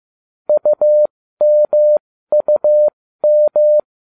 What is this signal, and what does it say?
Morse code, heard as a keyed tone, about 15 wpm, spaced as for 13 wpm, 600 Hz. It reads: UMUM